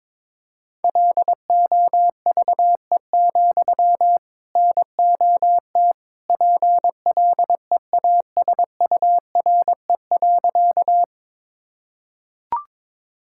Morse code, heard as a keyed tone, about 22 wpm, 700 Hz. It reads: LOVE, NOT PLEASURE.
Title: Sentences from Top 1600 Words 22wpm